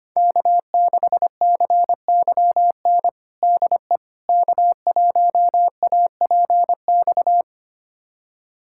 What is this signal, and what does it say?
Morse code, heard as a keyed tone, about 25 wpm, 700 Hz. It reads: K6CYN DE K1APX
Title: USA Callsign DE Callsign - 5 Characters - 25wpm